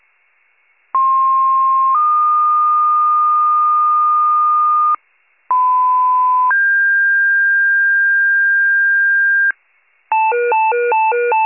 Two-tone page, heard as a one-second tone, then a three-second tone, about 1060 then 1220 Hz, about 1010 then 1600 Hz, with noise. It is heard from 0.9 to 9.5 s.